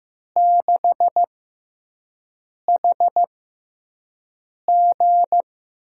Morse code, heard as a keyed tone, about 15 wpm, spaced as for 5 wpm, 700 Hz. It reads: 6HG
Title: USA Call Signs - 6x Character Spacing - 15wpm